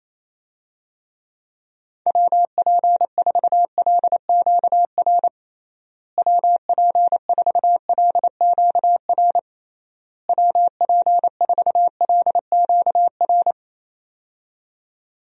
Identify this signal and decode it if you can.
Morse code, heard as a keyed tone, about 28 wpm, 700 Hz. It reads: WP4LQR WP4LQR WP4LQR
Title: USA Call Signs - 6 Characters - Repeated 3x - 28wpm